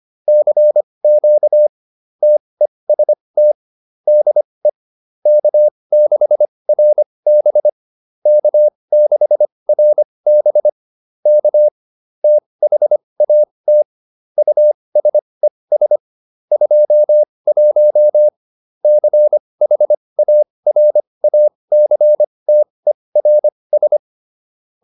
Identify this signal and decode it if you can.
Morse code, heard as a keyed tone, about 25 wpm, spaced as for 20 wpm, 600 Hz. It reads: CQ TEST DE K6RB K6RB K THAT USES 21 CHARACTERS